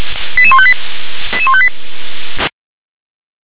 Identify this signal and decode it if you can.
Five-tone selective call (ZVEI), heard as quick runs of five tones, tones of about 70 ms. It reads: 88158 88158
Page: ZVEI